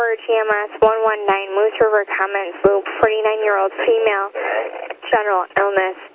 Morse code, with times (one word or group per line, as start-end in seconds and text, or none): none